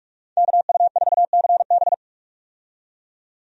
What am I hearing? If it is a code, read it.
Morse code, heard as a keyed tone, about 45 wpm, 700 Hz. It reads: KU4CB
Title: USA Call Signs - 5 Characters - 45wpm